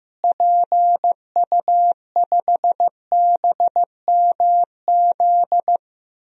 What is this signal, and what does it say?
Morse code, heard as a keyed tone, about 16 wpm, 700 Hz. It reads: PU5BMZ
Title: DXCC-100 Call Signs - 6 Characters - Repeated 3x - 15wpm